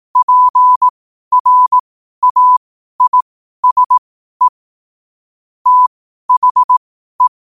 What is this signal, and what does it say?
Morse code, heard as a keyed tone, about 18 wpm, spaced as for 12 wpm, 1000 Hz. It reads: PRAISE THE
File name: qotdfiles / 12wpm